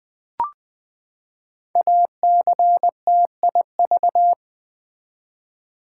Morse code, heard as a keyed tone, about 20 wpm, 700 Hz. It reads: ACTIV
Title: Build A Word - 5 Letters At A Time - 10-Letter Words - 20wpm